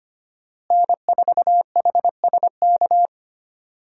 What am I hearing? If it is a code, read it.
Morse code, heard as a keyed tone, about 25 wpm, 700 Hz. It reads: N4HSK
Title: USA Call Signs - 5 Characters - Repeated 3x - 25wpm